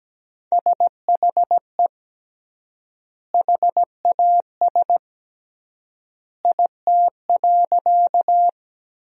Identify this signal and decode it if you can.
Morse code, heard as a keyed tone, about 17 wpm, 700 Hz. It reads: SHE HAS IT.
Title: 3x Word-Spacing - Sentences from Top 100 Words - 17wpm